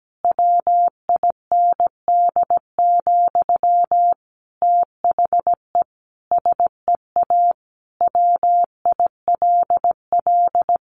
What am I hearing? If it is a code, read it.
Morse code, heard as a keyed tone, about 17 wpm, 700 Hz. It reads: WIND, THE SEA WILL
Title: Sentences from Top 1400 Words 17wpm